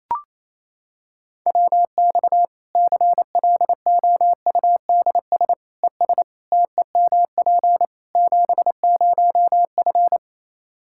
Morse code, heard as a keyed tone, about 28 wpm, 700 Hz. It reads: WX CLOUDS ES TEMP 70F
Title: Practice QSO - Exchange 2 - Info - 28wpm